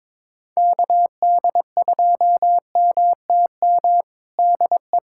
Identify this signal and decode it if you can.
Morse code, heard as a keyed tone, about 22 wpm, 700 Hz. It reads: KD2MTM DE
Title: USA Callsign DE Callsign - 6 Characters - 22wpm